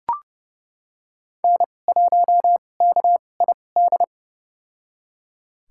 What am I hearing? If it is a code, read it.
Morse code, heard as a keyed tone, about 31 wpm, spaced as for 22 wpm, 700 Hz. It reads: N1KID
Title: USA Call Signs - 2x Character Spacing - 30wpm